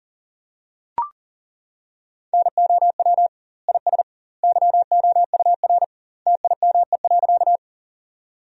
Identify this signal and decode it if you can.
Morse code, heard as a keyed tone, about 40 wpm, 700 Hz. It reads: NOW IS YOUR TIME.
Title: 2x Word-Spacing - Sentences from Top 100 Words - 40wpm